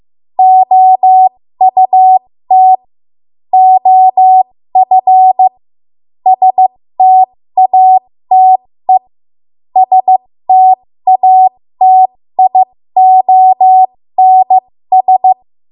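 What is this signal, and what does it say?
Morse code, heard as a keyed tone, about 15 wpm, spaced as for 13 wpm, 750 Hz. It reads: OUT OF STATE STATIONS
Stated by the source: 13 wpm Morse code practice archive transmission.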